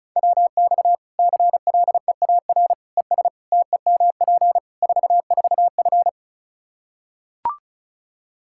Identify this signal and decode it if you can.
Morse code, heard as a keyed tone, about 34 wpm, 700 Hz. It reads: WX CLEAR ES TEMP 44F
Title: Practice QSO - Exchange 2 - Info - 35wpm